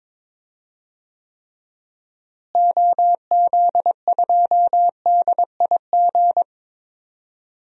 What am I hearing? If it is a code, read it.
Morse code, heard as a keyed tone, about 22 wpm, 700 Hz. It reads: OZ2DIG
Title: DXCC-100 Call Signs - 6 Characters - Repeated 2x - 22wpm